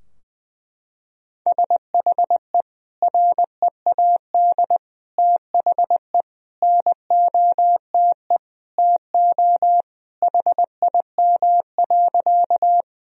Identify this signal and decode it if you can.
Morse code, heard as a keyed tone, about 20 wpm, 700 Hz. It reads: SHE READ THE NOTE TO HIM.
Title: Sentences from Top 1000 Words - Review Part 2 - 20wpm